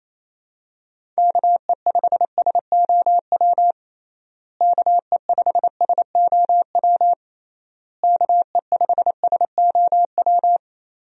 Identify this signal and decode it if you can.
Morse code, heard as a keyed tone, about 28 wpm, 700 Hz. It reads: KE5SOW KE5SOW KE5SOW
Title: USA Call Signs - 6 Characters - Repeated 3x - 28wpm